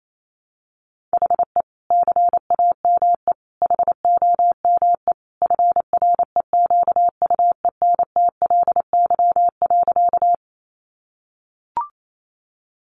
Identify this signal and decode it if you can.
Morse code, heard as a keyed tone, about 28 wpm, 700 Hz. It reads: HE CAME HOME FREQUENTLY.
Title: Sentences from Top 1300 Words 28wpm